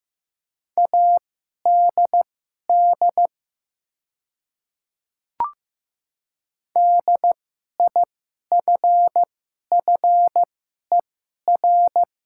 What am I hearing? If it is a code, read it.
Morse code, heard as a keyed tone, about 15 wpm, spaced as for 11 wpm, 690 Hz. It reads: ADD DIFFER
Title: 2x Character Spacing - Top 300 Words - 15wpm